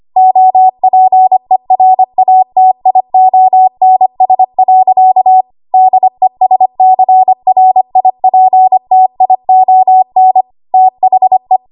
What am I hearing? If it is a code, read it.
Morse code, heard as a keyed tone, about 25 wpm, 750 Hz. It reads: OPERATIONS. DESCRIPTION THE
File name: Archive / 25 WPM / 25WPM